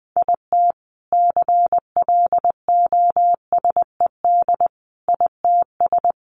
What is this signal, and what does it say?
Morse code, heard as a keyed tone, about 20 wpm, 700 Hz. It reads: IT CLOSED ITS
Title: A Sound Of Thunder 20wpm